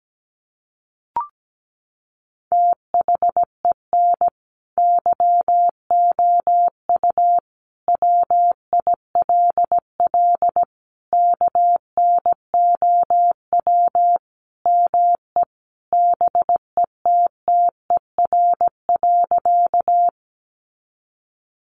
Morse code, heard as a keyed tone, about 17 wpm, 700 Hz. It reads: THEN YOU WILL KNOW ME BETTER.
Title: Sentences from Top 400 Words 17wpm